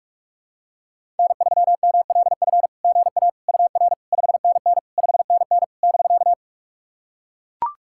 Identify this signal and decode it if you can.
Morse code, heard as a keyed tone, about 44 wpm, 660 Hz. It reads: N3MLF GA UR 5NN 5NN <BK>
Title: SOTA QSO Practice 45wpm